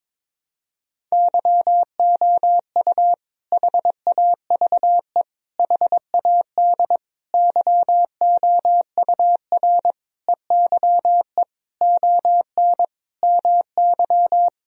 Morse code, heard as a keyed tone, about 22 wpm, 700 Hz. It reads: YOU HAVE HAD YOUR EYE ON MY
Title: Sentences from Top 1000 Words - Review Part 1 - 22wpm